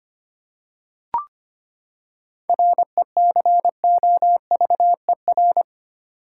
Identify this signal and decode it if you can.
Morse code, heard as a keyed tone, about 25 wpm, 700 Hz. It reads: RECOVER
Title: Seven Letter Words - 25wpm